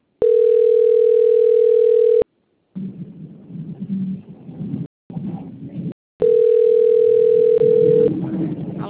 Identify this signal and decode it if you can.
Ringback tone (North America), 440 + 480 Hz.
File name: out- - - - - .866